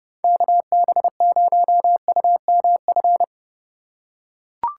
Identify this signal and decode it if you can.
Morse code, heard as a keyed tone, about 30 wpm, 700 Hz. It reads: KB0UMF
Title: USA Call Signs - 6 Characters - 3x Character Spacing - 30wpm